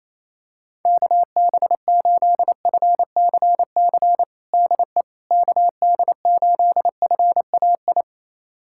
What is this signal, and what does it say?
Morse code, heard as a keyed tone, about 28 wpm, 700 Hz. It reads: KB8FCC DE KD8FAI
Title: USA Callsign DE Callsign 28wpm